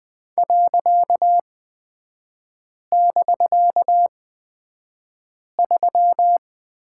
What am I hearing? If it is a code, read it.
Morse code, heard as a keyed tone, about 20 wpm, spaced as for 5 wpm, 700 Hz. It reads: .<BK>3
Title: 255 Lesson 41e - Period - Three Characters 20wpm